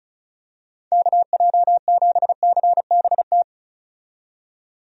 Morse code, heard as a keyed tone, about 34 wpm, 700 Hz. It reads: KJ7CBT
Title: USA Call Signs - 6 Characters - 35wpm